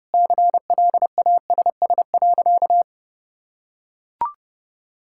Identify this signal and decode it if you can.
Morse code, heard as a keyed tone, about 32 wpm, 700 Hz. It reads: CLASS.
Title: Sentences from Top 1000 Words - Review Part 6 - 30wpm